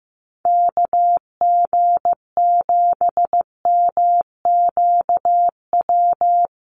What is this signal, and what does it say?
Morse code, heard as a keyed tone, about 15 wpm, 700 Hz. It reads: KG7MQW
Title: USA Call Signs Repeated 2x 15wpm